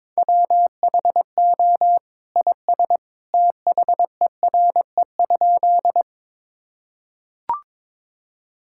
Morse code, heard as a keyed tone, about 22 wpm, 700 Hz. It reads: WHO IS THERE?
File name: practice / 22wpm / Sentences with 2-Word Phrases - 22wpm